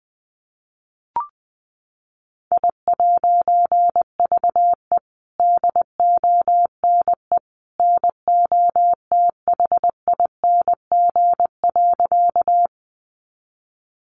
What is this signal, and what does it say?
Morse code, heard as a keyed tone, about 20 wpm, 700 Hz. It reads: I'VE DONE NOTHING.
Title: A Sound Of Thunder 20wpm